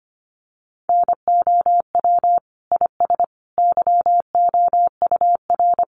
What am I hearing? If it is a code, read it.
Morse code, heard as a keyed tone, about 25 wpm, 700 Hz. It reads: NOW IS YOUR
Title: Sentences from Top 100 Words 25wpm